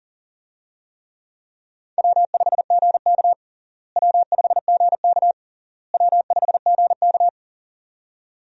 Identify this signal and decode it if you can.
Morse code, heard as a keyed tone, about 40 wpm, 700 Hz. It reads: W5GK W5GK W5GK
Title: USA Call Signs - 4 Characters - Repeated 3x - 40wpm